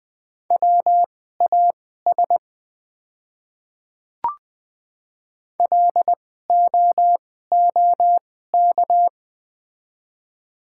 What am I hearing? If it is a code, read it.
Morse code, heard as a keyed tone, about 20 wpm, spaced as for 15 wpm, 700 Hz. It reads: WAS LOOK
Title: 2x Character Spacing - Top 100 Words - 20wpm